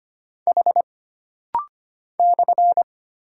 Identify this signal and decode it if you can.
Morse code, heard as a keyed tone, about 25 wpm, 700 Hz. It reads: H /
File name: Single Character (Letter:Number:Punctuation:Prosign) - Rapid-Fire - 25wpm